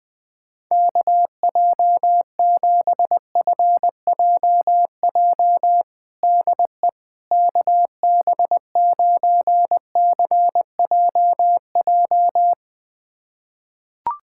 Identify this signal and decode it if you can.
Morse code, heard as a keyed tone, about 20 wpm, 700 Hz. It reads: KJ7FJJ DE KB9CJJ
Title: USA Callsign DE Callsign - 6 Characters - 20wpm